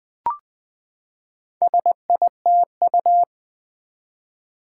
Morse code, heard as a keyed tone, about 20 wpm, 700 Hz. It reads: SITU
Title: Build A Word - 4 Letters At A Time - 8-Letter Words - 20wpm